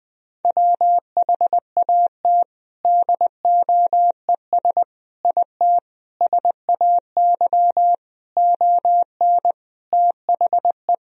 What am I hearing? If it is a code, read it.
Morse code, heard as a keyed tone, about 20 wpm, 700 Hz. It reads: WHAT DOES IT SAY ON THE
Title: Sentences from Top 1000 Words - Review Part 2 - 20wpm